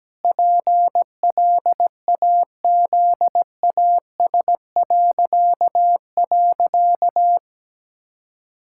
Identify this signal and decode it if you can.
Morse code, heard as a keyed tone, about 17 wpm, 700 Hz. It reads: PLAZAS..
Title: Pangram Challenge v2 - 17wpm